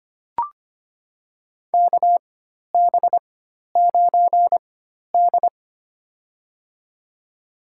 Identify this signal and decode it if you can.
Morse code, heard as a keyed tone, about 25 wpm, spaced as for 12 wpm, 700 Hz. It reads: KB9D
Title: USA Call Signs - 4x Character Spacing - 25wpm